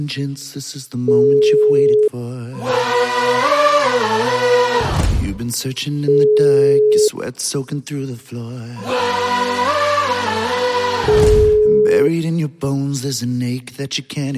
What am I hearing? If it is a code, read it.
Ringback tone (Europe, 425 Hz), with speech.